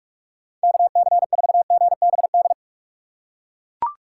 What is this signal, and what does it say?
Morse code, heard as a keyed tone, about 43 wpm, 700 Hz. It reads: KC4GBD
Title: USA Call Signs - 2x Character Spacing - 45wpm